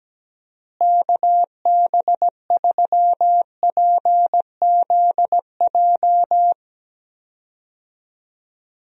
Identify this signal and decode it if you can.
Morse code, heard as a keyed tone, about 17 wpm, 700 Hz. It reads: KB3PZJ
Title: USA Call Signs - 6 Characters - Repeated 2x - 17wpm